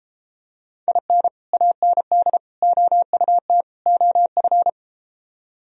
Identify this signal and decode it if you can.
Morse code, heard as a keyed tone, about 34 wpm, 700 Hz. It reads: IN AND OUT OF